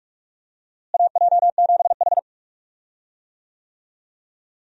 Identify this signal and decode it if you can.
Morse code, heard as a keyed tone, about 48 wpm, 700 Hz. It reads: AJ7H
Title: USA Call Signs - 4 Characters - Repeated 2x - 45wpm